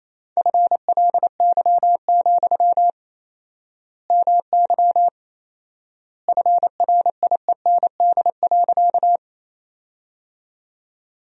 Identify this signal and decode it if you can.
Morse code, heard as a keyed tone, about 28 wpm, 700 Hz. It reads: FLY, MY FRIEND.
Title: 4x Word-Spacing - Sentences from Top 500 Words - 28wpm